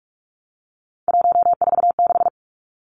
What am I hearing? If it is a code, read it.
Morse code, heard as a keyed tone, about 44 wpm, 700 Hz. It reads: 146
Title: Numbers - 3-Digit 45wpm